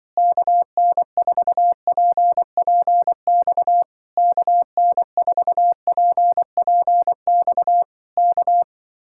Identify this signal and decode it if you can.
Morse code, heard as a keyed tone, about 23 wpm, 700 Hz. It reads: KN4PPX KN4PPX K
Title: Speed-Racing Common QSO Phrases - 20wpm